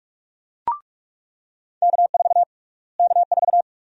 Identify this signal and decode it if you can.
Morse code, heard as a keyed tone, about 46 wpm, 700 Hz. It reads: K4 K4
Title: USA Call Sign Prefixes - Repeated 2x - 45wpm